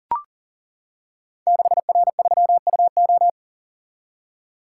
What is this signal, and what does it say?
Morse code, heard as a keyed tone, about 38 wpm, 700 Hz. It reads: 6R3UO
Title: Code Groups - 5 Letter:Numbers - 40wpm